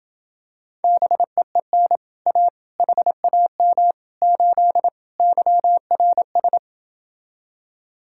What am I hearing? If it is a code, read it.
Morse code, heard as a keyed tone, about 28 wpm, 700 Hz. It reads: BEEN A HAM 8 YRS